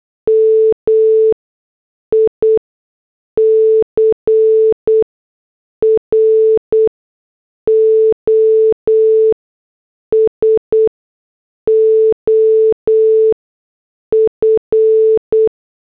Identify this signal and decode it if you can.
Morse code, heard as a keyed tone, about 8 wpm, 440 Hz. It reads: M I C R O S O F
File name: morse / headlines